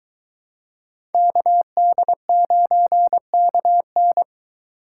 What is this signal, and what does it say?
Morse code, heard as a keyed tone, about 22 wpm, 700 Hz. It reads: KD9KN